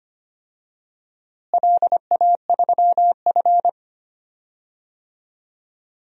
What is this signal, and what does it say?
Morse code, heard as a keyed tone, about 25 wpm, 700 Hz. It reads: LA3F
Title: DXCC-100 Call Signs - 4 Characters - Repeated 2x - 25wpm